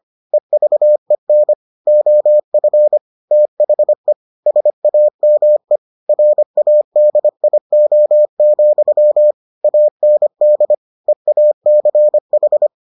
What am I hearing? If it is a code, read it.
Morse code, heard as a keyed tone, about 25 wpm, 600 Hz. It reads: EVEN OF THE SAME RADIO, AND EACH